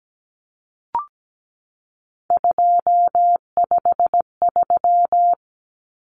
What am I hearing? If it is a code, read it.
Morse code, heard as a keyed tone, about 17 wpm, 700 Hz. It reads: 253
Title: Numbers - 3-Digit 17wpm